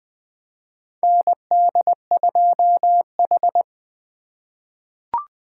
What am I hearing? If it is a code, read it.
Morse code, heard as a keyed tone, about 21 wpm, 700 Hz. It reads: ND2H
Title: USA Call Signs - 3x Character Spacing - 20wpm